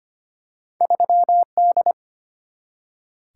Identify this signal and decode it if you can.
Morse code, heard as a keyed tone, about 25 wpm, 700 Hz. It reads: 3D
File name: Two Letter-Number Combinations - 25wpm